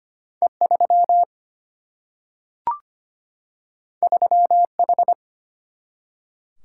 Morse code, heard as a keyed tone, about 25 wpm, 700 Hz. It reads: E3 3H
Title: Single-Letter Single-Number 25wpm